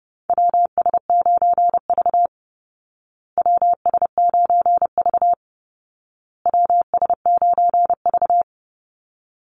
Morse code, heard as a keyed tone, about 30 wpm, 700 Hz. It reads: WS9V WS9V WS9V
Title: Call Signs Repeated 2x - 30wpm